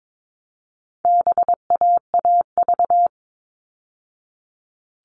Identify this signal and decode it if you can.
Morse code, heard as a keyed tone, about 22 wpm, 700 Hz. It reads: BAAV
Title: All Cut Numbers - 4-Digit 22wpm